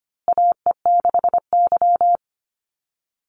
Morse code, heard as a keyed tone, about 25 wpm, 700 Hz. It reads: AE6Y
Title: Call Signs Repeated 2x - 25wpm